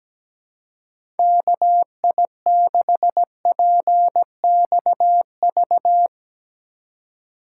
Morse code, heard as a keyed tone, about 17 wpm, 700 Hz. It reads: KI6PXV
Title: USA Call Signs - 6 Characters - 2x Character Spacing - 17wpm